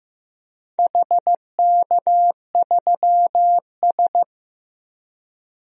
Morse code, heard as a keyed tone, about 15 wpm, 700 Hz. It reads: HK3S
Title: DXCC-100 Call Signs - 4 Characters - 15wpm